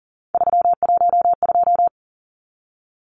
Morse code, heard as a keyed tone, about 40 wpm, 700 Hz. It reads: 312
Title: Numbers - 3-Digit 40wpm